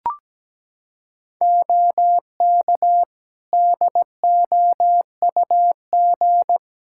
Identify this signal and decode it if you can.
Morse code, heard as a keyed tone, about 17 wpm, 700 Hz. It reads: OK DOUG
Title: Practice QSO - Exchange 2 - Info - 17wpm